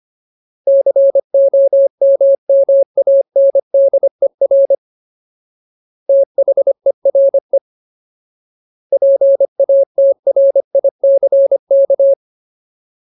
Morse code, heard as a keyed tone, about 25 wpm, 550 Hz. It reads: COMMANDER THERE PATRICK
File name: Ultimate Word List / 25WPM Ultimate